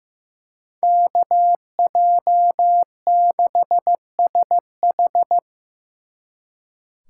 Morse code, heard as a keyed tone, about 18 wpm, 700 Hz. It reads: KJ6SH
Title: Speed Racing - USA 2x2 Call Signs 15wpm